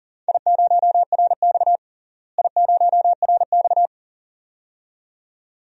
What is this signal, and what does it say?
Morse code, heard as a keyed tone, about 40 wpm, 700 Hz. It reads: I0RX I0RX
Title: DXCC-100 Call Signs - 4 Characters - Repeated 2x - 40wpm